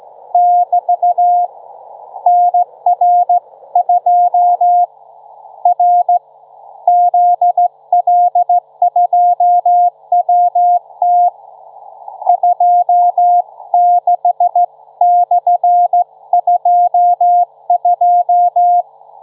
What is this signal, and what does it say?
Morse code, heard as a keyed tone, about 16 wpm, 710 Hz. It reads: <BT> NR2 R ZL2WT 26/22
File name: Trivia-ZL2WT